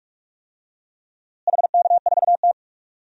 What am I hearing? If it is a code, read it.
Morse code, heard as a keyed tone, about 46 wpm, 700 Hz. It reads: HK4T
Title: DXCC-100 Call Signs - 4 Characters - Repeated 2x - 45wpm